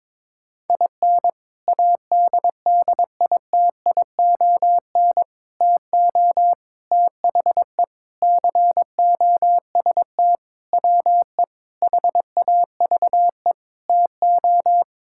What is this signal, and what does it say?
Morse code, heard as a keyed tone, about 22 wpm, 700 Hz. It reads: IN ADDITION TO THE COST WE HAVE TO
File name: practice / 22wpm / Sentences with 4-Word Phrases - 22wpm